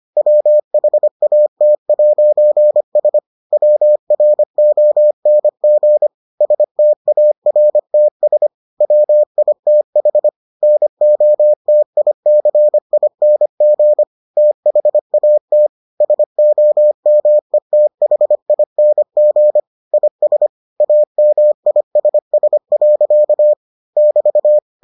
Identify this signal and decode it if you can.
Morse code, heard as a keyed tone, about 25 wpm, 600 Hz. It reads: WHAT'S WRONG STARTS WITH NOTICING THAT SOMETHING IS AMISS. <BT>